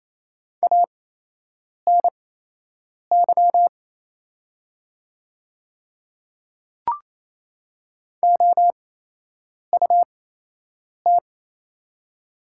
Morse code, heard as a keyed tone, about 28 wpm, spaced as for 8 wpm, 700 Hz. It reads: ANY OUT
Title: 8x Character Spacing - Top 200 Words - 28wpm